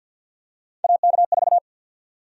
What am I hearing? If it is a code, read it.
Morse code, heard as a keyed tone, about 51 wpm, 700 Hz. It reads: AK4
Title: USA Call Sign Prefixes - 50wpm